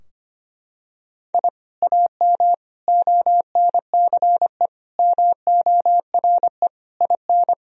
Morse code, heard as a keyed tone, about 25 wpm, 700 Hz. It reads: I AM ONCE MORE IN
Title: Sentences from Top 1000 Words - Review Part 3 - 25wpm